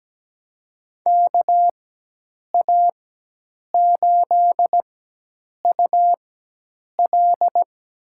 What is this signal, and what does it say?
Morse code, heard as a keyed tone, about 17 wpm, spaced as for 8 wpm, 700 Hz. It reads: KA8UL
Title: USA Call Signs - 4x Character Spacing - 17wpm